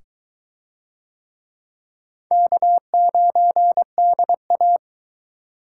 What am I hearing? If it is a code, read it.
Morse code, heard as a keyed tone, about 21 wpm, 700 Hz. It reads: K9DA